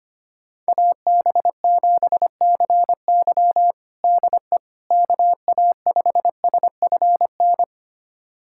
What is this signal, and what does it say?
Morse code, heard as a keyed tone, about 25 wpm, 700 Hz. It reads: AB7CY DE KA5SFN
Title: USA Callsign DE Callsign 25wpm